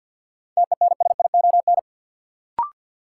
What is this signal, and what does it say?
Morse code, heard as a keyed tone, about 47 wpm, 700 Hz. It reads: TENSION
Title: Sentences from Top 1600 Words - New Word Drill - 50wpm